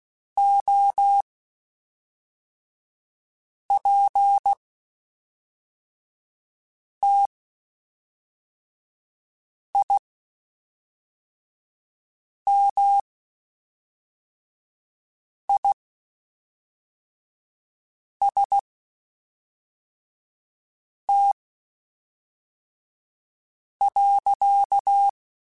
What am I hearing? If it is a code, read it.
Morse code, heard as a keyed tone, about 16 wpm, spaced as for 3 wpm, 800 Hz. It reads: OPTIMIST.
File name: Quotes Spaced / Quotes Spaced-mp3s / Optimist sp @5